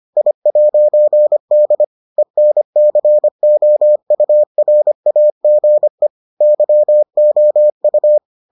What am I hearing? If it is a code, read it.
Morse code, heard as a keyed tone, about 25 wpm, 600 Hz. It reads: I'D ENCOURAGE YOU